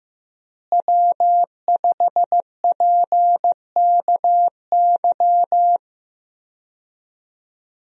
Morse code, heard as a keyed tone, about 15 wpm, 700 Hz. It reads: W5PKY
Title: USA Call Signs - 5 Characters - Repeated 3x - 15wpm